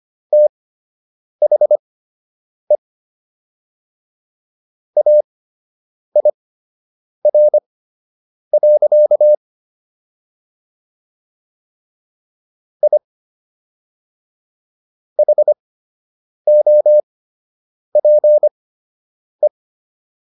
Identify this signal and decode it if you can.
Morse code, heard as a keyed tone, about 25 wpm, spaced as for 8 wpm, 600 Hz. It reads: THE AIR. I HOPE